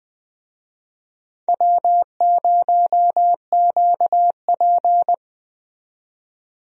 Morse code, heard as a keyed tone, about 20 wpm, 700 Hz. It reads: W0QP
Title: USA Call Signs - 4 Characters - 20wpm